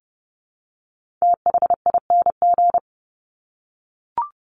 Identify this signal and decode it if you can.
Morse code, heard as a keyed tone, about 30 wpm, 700 Hz. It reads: THING
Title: Top 300 Words 30wpm